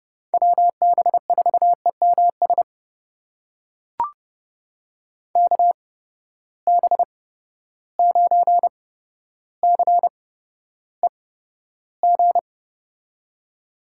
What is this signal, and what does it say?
Morse code, heard as a keyed tone, about 30 wpm, spaced as for 13 wpm, 700 Hz. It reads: WB4EMS KB9CEG
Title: USA Call Signs - 6 Characters - 8x Character Spacing - 30wpm